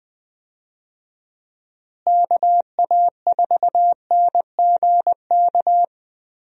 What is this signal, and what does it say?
Morse code, heard as a keyed tone, about 20 wpm, 700 Hz. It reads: KA4NGK
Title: USA Call Signs - 6 Characters - Repeated 3x - 20wpm